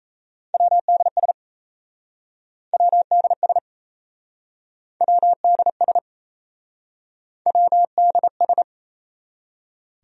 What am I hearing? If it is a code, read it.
Morse code, heard as a keyed tone, about 37 wpm, 700 Hz. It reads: WDS WDS WDS WDS